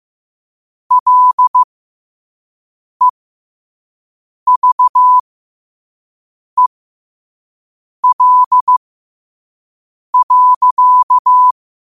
Morse code, heard as a keyed tone, about 15 wpm, spaced as for 5 wpm, 1000 Hz. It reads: LEVEL.
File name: qotdfiles / 5wpm